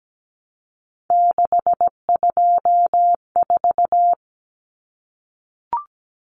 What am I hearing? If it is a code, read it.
Morse code, heard as a keyed tone, about 17 wpm, 700 Hz. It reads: 624
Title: Numbers - 3-Digit 17wpm